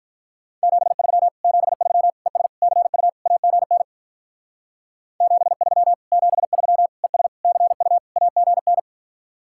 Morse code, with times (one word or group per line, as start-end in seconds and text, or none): 0.6-1.3 73
1.4-2.1 73
2.3-2.5 ES
2.6-3.1 CU
3.3-3.8 AGN
5.2-5.9 73
6.1-6.9 73
7.0-7.3 ES
7.4-8.0 CU
8.2-8.8 AGN